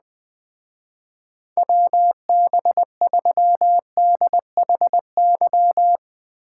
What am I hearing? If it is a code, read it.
Morse code, heard as a keyed tone, about 20 wpm, 700 Hz. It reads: WB3DHY
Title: Speed Racing - USA Call Signs 17wpm